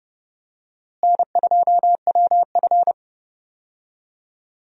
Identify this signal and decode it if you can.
Morse code, heard as a keyed tone, about 30 wpm, 700 Hz. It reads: N2WF
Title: USA Call Signs - 4 Characters - Repeated 2x - 30wpm